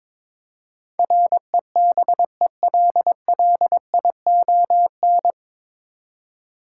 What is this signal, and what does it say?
Morse code, heard as a keyed tone, about 22 wpm, 700 Hz. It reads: REBELLION
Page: Morse Code Ninja practice 22wpm